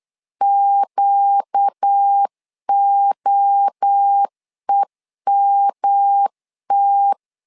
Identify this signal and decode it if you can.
Morse code, heard as a keyed tone, about 8 wpm, 800 Hz. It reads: QOEMT
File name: 4 Q 8wpm